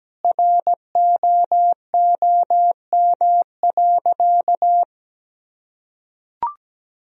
Morse code, heard as a keyed tone, about 17 wpm, 700 Hz. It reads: ROOM.
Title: Sentences from Top 500 Words and Abbreviations 17wpm